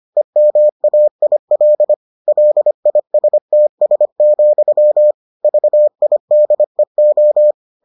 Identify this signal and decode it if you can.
Morse code, heard as a keyed tone, about 25 wpm, 600 Hz. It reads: EMAIL LISTS, VIDEO